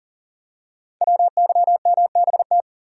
Morse code, heard as a keyed tone, about 39 wpm, 700 Hz. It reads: WYMBT
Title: Code Groups - 5 Letters - 40wpm